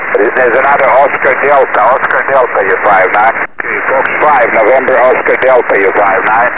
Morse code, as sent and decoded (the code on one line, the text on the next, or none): none